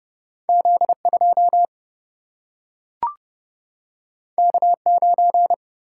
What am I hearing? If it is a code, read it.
Morse code, heard as a keyed tone, about 30 wpm, 700 Hz. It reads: Z 2 K 9
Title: Single-Letter Single-Number 30wpm